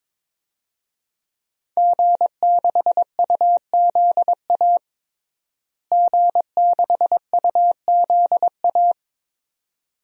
Morse code, heard as a keyed tone, about 22 wpm, 700 Hz. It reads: G6UZA G6UZA
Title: DXCC-100 Call Signs - 5 Characters - Repeated 2x - 22wpm